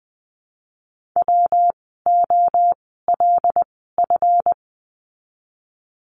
Morse code, heard as a keyed tone, about 20 wpm, 700 Hz. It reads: W O L F
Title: 139 Lesson 23f - Letter f - Words 20wpm